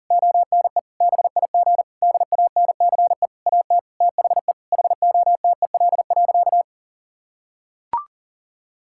Morse code, heard as a keyed tone, about 40 wpm, 700 Hz. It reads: ONE BIG DANCE AT THE HOTEL.
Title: Sentences from Top 1600 Words 40wpm